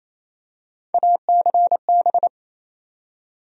Morse code, heard as a keyed tone, about 28 wpm, 700 Hz. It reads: ACB
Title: USA Call Sign Postfixes - 3 Letters 28wpm